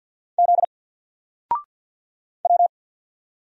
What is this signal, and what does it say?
Morse code, heard as a keyed tone, about 50 wpm, 700 Hz. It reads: Z W
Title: Single Letter-Number Alt 50wpm